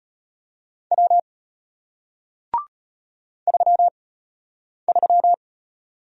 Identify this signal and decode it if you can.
Morse code, heard as a keyed tone, about 33 wpm, 700 Hz. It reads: W 3 3